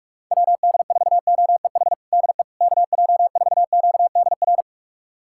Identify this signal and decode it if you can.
Morse code, heard as a keyed tone, about 46 wpm, 700 Hz. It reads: WD4OEH DE KJ4QDR
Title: USA Callsign DE Callsign - 6 Characters - 45wpm